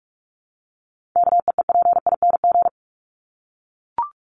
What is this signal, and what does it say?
Morse code, heard as a keyed tone, about 44 wpm, 700 Hz. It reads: KEEPING
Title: ING Suffix - Top 100 Words with Base Words 45wpm